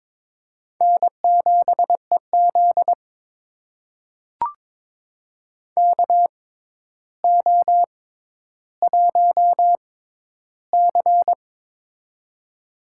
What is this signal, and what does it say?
Morse code, heard as a keyed tone, about 22 wpm, spaced as for 11 wpm, 700 Hz. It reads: N7EZ KO1C
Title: USA Call Signs - 4 Characters - 6x Character Spacing - 22wpm